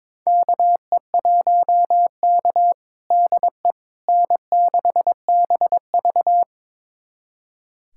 Morse code, heard as a keyed tone, about 22 wpm, 700 Hz. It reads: KE1K DE N6BV
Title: USA Callsign DE Callsign - 4 Characters - 22wpm